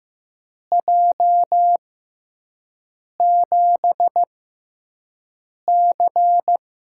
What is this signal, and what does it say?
Morse code, heard as a keyed tone, about 15 wpm, spaced as for 5 wpm, 700 Hz. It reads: J7C
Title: USA Call Signs - 6x Character Spacing - 15wpm